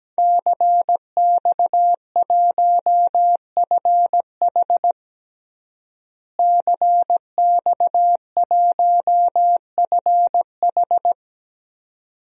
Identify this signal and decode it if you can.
Morse code, heard as a keyed tone, about 17 wpm, 700 Hz. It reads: CX1FH CX1FH
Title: DXCC-100 Call Signs - 5 Characters - Repeated 2x - 17wpm